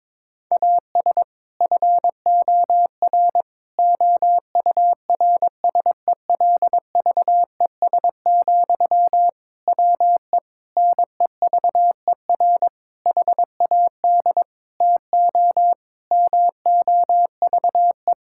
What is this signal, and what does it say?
Morse code, heard as a keyed tone, about 22 wpm, 700 Hz. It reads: AS FOR OURSELVES, WE NEVER HAD TO MOVE
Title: Sentences from Top 1600 Words 22wpm